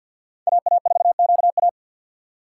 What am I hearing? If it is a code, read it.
Morse code, heard as a keyed tone, about 49 wpm, 700 Hz. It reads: AA4QA
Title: USA Call Signs - 5 Characters - 4x Character Spacing - 50wpm